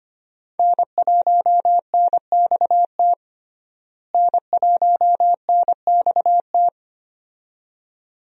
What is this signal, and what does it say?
Morse code, heard as a keyed tone, about 25 wpm, 700 Hz. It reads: N1NXT N1NXT
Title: USA Call Signs - 5 Characters - Repeated 2x - 25wpm